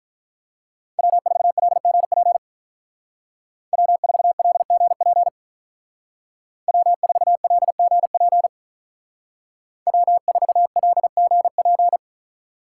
Morse code, heard as a keyed tone, about 45 wpm, 700 Hz. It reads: W4LGP W4LGP W4LGP W4LGP